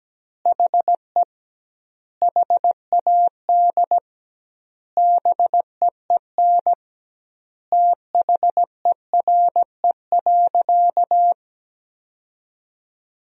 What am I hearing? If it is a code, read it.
Morse code, heard as a keyed tone, about 17 wpm, 700 Hz. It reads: HE HAD BEEN THERE.
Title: 2x Word-Spacing - Sentences from Top 300 Words - 17wpm